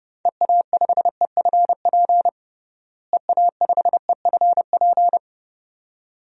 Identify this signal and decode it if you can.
Morse code, heard as a keyed tone, about 30 wpm, 700 Hz. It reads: EA5EFP EA5EFP
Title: DXCC-100 Call Signs - 6 Characters - Repeated 2x - 30wpm